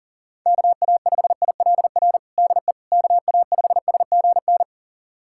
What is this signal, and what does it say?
Morse code, heard as a keyed tone, about 40 wpm, 700 Hz. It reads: KA5ILR DE KA5SGN
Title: USA Callsign DE Callsign - 6 Characters - 40wpm